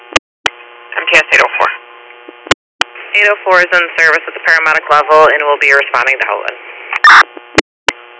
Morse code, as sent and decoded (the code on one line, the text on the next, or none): none